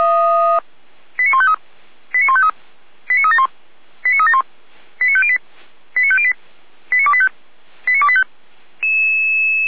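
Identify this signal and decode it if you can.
Five-tone selective call (ZVEI), heard as quick runs of five tones, tones of about 70 ms. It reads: 89153 89153 89371 89371 89508 89508 89285 89285
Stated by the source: ZVEI five-tone selective call received by SDR.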